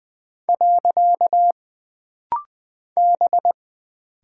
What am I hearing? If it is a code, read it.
Morse code, heard as a keyed tone, about 20 wpm, 700 Hz. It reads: . B
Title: Single Character (Letter:Number:Punctuation:Prosign) - Rapid-Fire - 20wpm